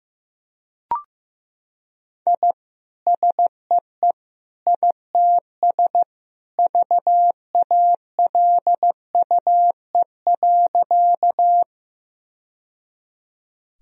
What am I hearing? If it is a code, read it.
Morse code, heard as a keyed tone, about 15 wpm, 700 Hz. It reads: I SEE ITS VALUE.
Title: Sentences from Top 1000 Words - Review Part 3 - 15wpm